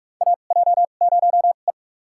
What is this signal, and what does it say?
Morse code, heard as a keyed tone, about 42 wpm, spaced as for 31 wpm, 700 Hz. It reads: AJ0E
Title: USA Call Signs - 4 Characters - 2x Character Spacing - 45wpm